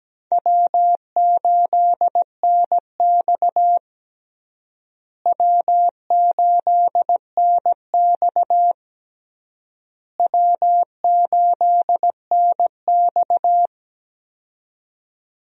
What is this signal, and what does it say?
Morse code, heard as a keyed tone, about 17 wpm, 700 Hz. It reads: W8NX W8NX W8NX
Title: USA Call Signs - 4 Characters - Repeated 3x - 17wpm